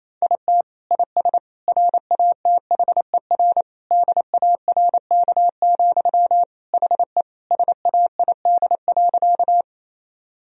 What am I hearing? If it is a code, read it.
Morse code, heard as a keyed tone, about 28 wpm, 700 Hz. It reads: IT IS RATHER DARK, HE SAID.
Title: Sentences from Top 1000 Words - Review Part 4 - 28wpm